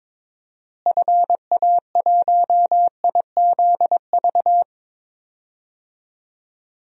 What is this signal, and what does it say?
Morse code, heard as a keyed tone, about 22 wpm, 700 Hz. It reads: FA1IZV
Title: DXCC-100 Call Signs - 6 Characters - Repeated 3x - 22wpm